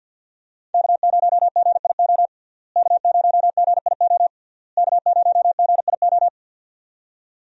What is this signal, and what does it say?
Morse code, heard as a keyed tone, about 52 wpm, 700 Hz. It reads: K0GIO K0GIO K0GIO
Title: USA Call Signs - 5 Characters - Repeated 3x - 50wpm